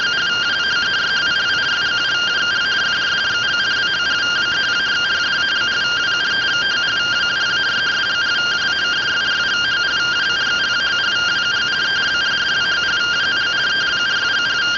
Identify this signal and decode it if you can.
Two-tone teleprinter signal, about 170 Hz shift.